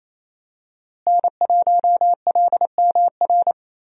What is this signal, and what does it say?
Morse code, heard as a keyed tone, about 28 wpm, 700 Hz. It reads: N1LMR
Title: USA Call Signs - 5 Characters - Repeated 2x - 28wpm